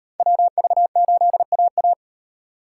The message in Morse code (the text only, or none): WV8AA